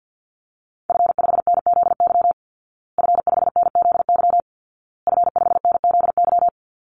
Morse code, heard as a keyed tone, about 50 wpm, 700 Hz. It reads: F5NZY F5NZY F5NZY
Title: Call Signs Repeated 2x - 50wpm